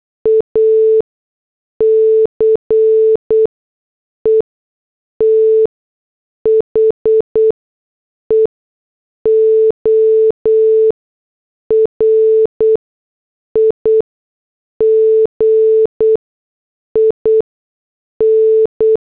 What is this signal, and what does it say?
Morse code, heard as a keyed tone, about 8 wpm, spaced as for 6 wpm, 440 Hz. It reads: ACE THE ORIGIN